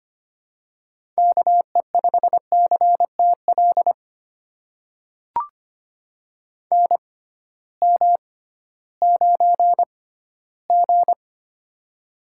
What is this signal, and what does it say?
Morse code, heard as a keyed tone, about 25 wpm, spaced as for 15 wpm, 700 Hz. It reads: KE5CTL NM9G
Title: USA Call Signs - 6x Character Spacing - 25wpm